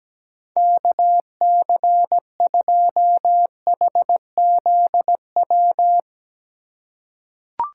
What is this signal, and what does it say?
Morse code, heard as a keyed tone, about 17 wpm, 700 Hz. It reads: KC2HZW
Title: USA Call Signs - 2x Character Spacing - 17wpm